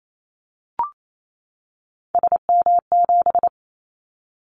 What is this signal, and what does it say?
Morse code, heard as a keyed tone, about 28 wpm, 700 Hz. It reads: SM7
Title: Callsign Prefixes for Top 25 DXCC - 28wpm